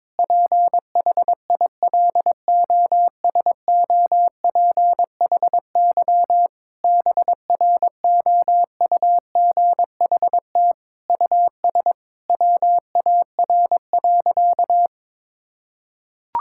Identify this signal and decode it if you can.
Morse code, heard as a keyed tone, about 22 wpm, 700 Hz. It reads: PHILOSOPHY BROUGHT US WAR.
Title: Sentences from Top 1000 Words - Review Part 6 - 22wpm